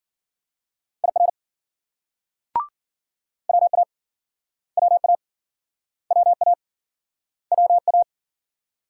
Morse code, heard as a keyed tone, about 51 wpm, 700 Hz. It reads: IR WA WA WA WA